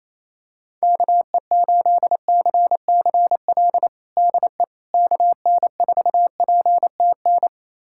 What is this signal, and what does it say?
Morse code, heard as a keyed tone, about 27 wpm, 700 Hz. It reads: KE8CCL DE KN4PTN
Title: USA Callsign DE Callsign - 6 Characters - 28wpm